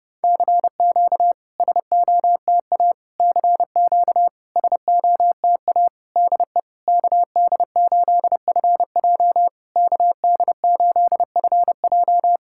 Morse code, heard as a keyed tone, about 31 wpm, 690 Hz. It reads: CQ SOTA CQ SOTA DE KD8FJ KD8FJ
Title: SOTA QSO Practice 30wpm